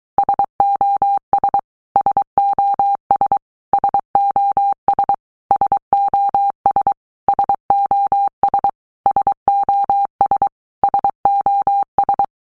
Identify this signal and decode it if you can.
Morse code, heard as a keyed tone, about 23 wpm, 800 Hz. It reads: SOS SOS SOS SOS SOS SOS SOS
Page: SOS sound in Morse Code (Fast)